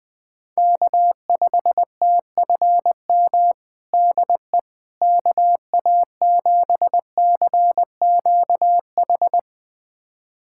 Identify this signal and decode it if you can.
Morse code, heard as a keyed tone, about 20 wpm, 700 Hz. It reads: K5TFM DE KA7CQH
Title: USA Callsign DE Callsign 20wpm